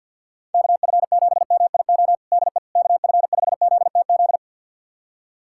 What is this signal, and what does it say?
Morse code, heard as a keyed tone, about 52 wpm, 700 Hz. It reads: KF7MIO DE KF5ZTZ
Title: USA Callsign DE Callsign 50wpm